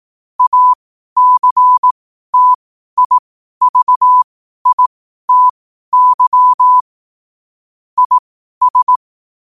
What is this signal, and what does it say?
Morse code, heard as a keyed tone, about 18 wpm, spaced as for 12 wpm, 1000 Hz. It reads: ACTIVITY IS
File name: qotdfiles / 12wpm